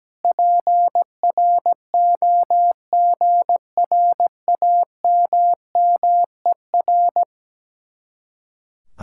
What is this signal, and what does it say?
Morse code, heard as a keyed tone, about 17 wpm, 700 Hz. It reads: PROGRAMMER
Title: Speed-Racing Common QSO Phrases - 17wpm